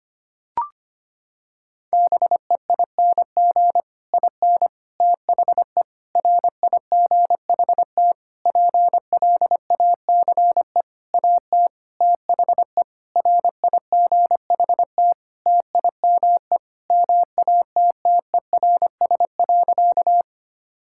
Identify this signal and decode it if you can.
Morse code, heard as a keyed tone, about 25 wpm, 700 Hz. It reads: BEING IN THE RIGHT PLACE AT THE RIGHT TIME MATTERS.
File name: practice / 25wpm / Sentences with 9-Word Phrases - 25wpm